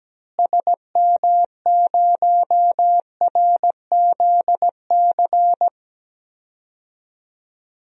Morse code, heard as a keyed tone, about 17 wpm, 700 Hz. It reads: SM0RZC
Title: DXCC-100 Call Signs - 6 Characters - Repeated 2x - 17wpm